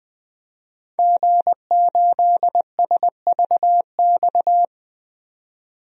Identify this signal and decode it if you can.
Morse code, heard as a keyed tone, about 20 wpm, 700 Hz. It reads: G8SVX
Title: DXCC-100 Call Signs - 5 Characters - 20wpm